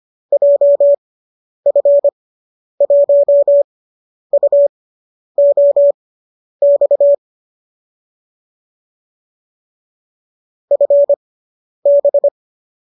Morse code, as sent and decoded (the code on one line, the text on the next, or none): .--- ..-. .---- ..- --- -..- / ..-. -...
JF1UOX FB